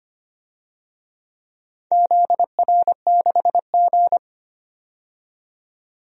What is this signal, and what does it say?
Morse code, heard as a keyed tone, about 25 wpm, 700 Hz. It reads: ZR6G
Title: DXCC-100 Call Signs - 4 Characters - 25wpm